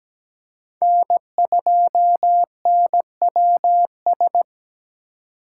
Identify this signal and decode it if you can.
Morse code, heard as a keyed tone, about 20 wpm, 700 Hz. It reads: N2NWS
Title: Speed Racing - USA 1x3 Call Signs 17wpm